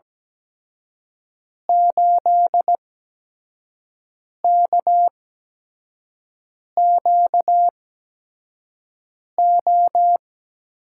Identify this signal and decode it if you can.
Morse code, heard as a keyed tone, about 17 wpm, spaced as for 5 wpm, 700 Hz. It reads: 8KQO
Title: USA Call Signs - 8x Character Spacing - 17wpm